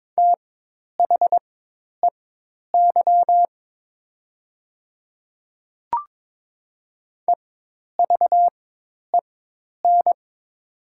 Morse code, heard as a keyed tone, about 22 wpm, spaced as for 10 wpm, 700 Hz. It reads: THEY EVEN
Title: 4x Character Spacing - Top 200 Words - 22wpm